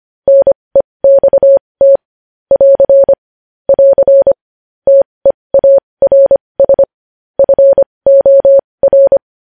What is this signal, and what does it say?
Morse code, heard as a keyed tone, about 25 wpm, spaced as for 20 wpm, 550 Hz. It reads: NEXT <AR> <AR> TEARS FOR
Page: news-headlines-in-morse-code-at-20-wpm